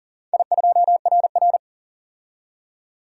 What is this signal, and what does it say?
Morse code, heard as a keyed tone, about 40 wpm, 700 Hz. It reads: I2RR
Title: DXCC-100 Call Signs - 4 Characters - 40wpm